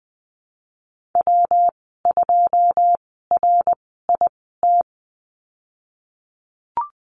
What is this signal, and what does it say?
Morse code, heard as a keyed tone, about 20 wpm, spaced as for 15 wpm, 700 Hz. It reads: W2RIT
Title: 122 Lesson 20g - Number 3 - Call Signs 20wpm